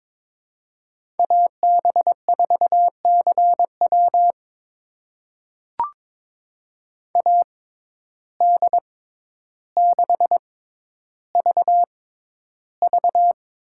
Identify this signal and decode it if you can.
Morse code, heard as a keyed tone, about 21 wpm, spaced as for 11 wpm, 700 Hz. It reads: AB4CW AD6VV
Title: USA Call Signs - 5 Characters - 6x Character Spacing - 22wpm